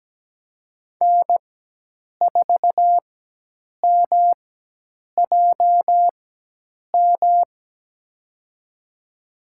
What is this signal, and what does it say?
Morse code, heard as a keyed tone, about 17 wpm, spaced as for 8 wpm, 700 Hz. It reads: N4MJM
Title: USA Call Signs - 5 Characters - 4x Character Spacing - 17wpm